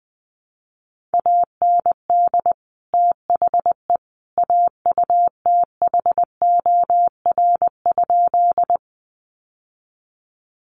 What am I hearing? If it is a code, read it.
Morse code, heard as a keyed tone, about 20 wpm, 700 Hz. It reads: AND THE AUTHOR?
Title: Sentences from Top 1200 Words 20wpm